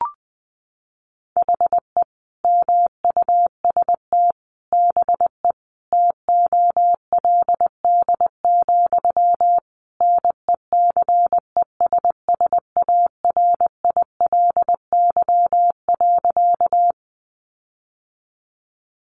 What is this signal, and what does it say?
Morse code, heard as a keyed tone, about 20 wpm, 700 Hz. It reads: HE MUST BE TOLD, NECESSARILY.
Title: Sentences from Top 1400 Words 20wpm